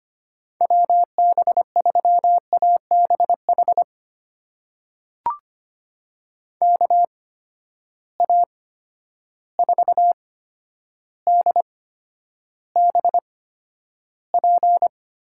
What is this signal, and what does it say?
Morse code, heard as a keyed tone, about 25 wpm, spaced as for 11 wpm, 700 Hz. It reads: WB3ABH KA4DBP
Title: USA Call Signs - 6 Characters - 8x Character Spacing - 25wpm